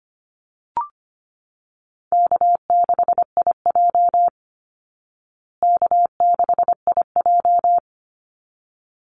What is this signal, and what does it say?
Morse code, heard as a keyed tone, about 25 wpm, 700 Hz. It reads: K6IJ K6IJ
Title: Easy Call Signs Repeated 2x - 25wpm